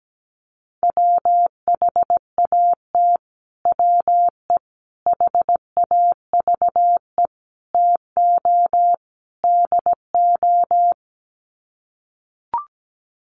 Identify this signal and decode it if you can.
Morse code, heard as a keyed tone, about 17 wpm, 700 Hz. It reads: WHAT WE HAVE TO DO
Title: Sets of 5 Words Encore 17wpm